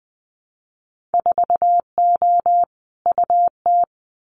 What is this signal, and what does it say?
Morse code, heard as a keyed tone, about 20 wpm, 700 Hz. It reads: 4O UT
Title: Winter Field Day Exchanges Encore 20wpm